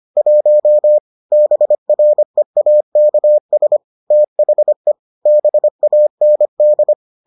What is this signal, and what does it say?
Morse code, heard as a keyed tone, about 25 wpm, 600 Hz. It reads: 1 BREAKS THE BAND